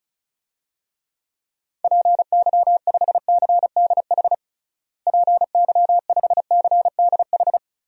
Morse code, heard as a keyed tone, about 34 wpm, 700 Hz. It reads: PY5CDH PY5CDH
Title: DXCC-100 Call Signs - 6 Characters - Repeated 3x - 35wpm